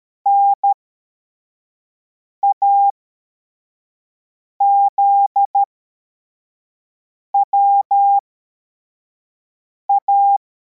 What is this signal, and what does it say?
Morse code, heard as a keyed tone, about 13 wpm, spaced as for 4 wpm, 800 Hz. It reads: NAZWA
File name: formation / cw / Text42